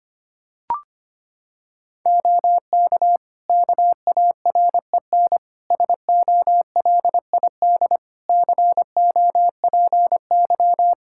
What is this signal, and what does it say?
Morse code, heard as a keyed tone, about 25 wpm, 700 Hz. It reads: OK KAREN SOLID COPY
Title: Practice QSO - Exchange 3 - More Info - 25wpm